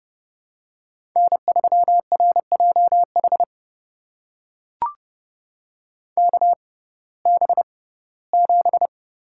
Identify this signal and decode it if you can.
Morse code, heard as a keyed tone, about 30 wpm, spaced as for 18 wpm, 700 Hz. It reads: N3RJH KB7
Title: USA Call Signs - 5 Characters - 6x Character Spacing - 30wpm